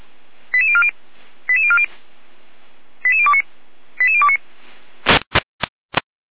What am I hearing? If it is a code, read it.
Five-tone selective call (ZVEI), heard as quick runs of five tones, tones of about 70 ms. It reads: 88040 88040 88029 88029
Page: /455/SDR/ZVEI/88402/